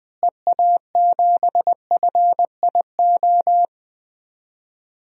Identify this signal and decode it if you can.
Morse code, heard as a keyed tone, about 20 wpm, 700 Hz. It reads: EA7FIO
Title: DXCC-100 Call Signs - 6 Characters - Repeated 2x - 20wpm